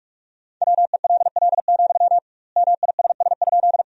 Morse code, heard as a keyed tone, about 47 wpm, 700 Hz. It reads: WELL, MISS?
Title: 2x Word-Spacing - Sentences from Top 500 Words - 45wpm